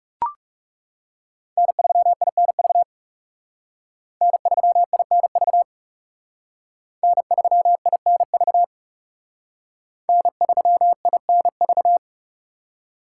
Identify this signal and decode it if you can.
Morse code, heard as a keyed tone, about 36 wpm, 700 Hz. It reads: N3INV N3INV N3INV N3INV
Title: Speed Racing - USA 1x3 Call Signs 30wpm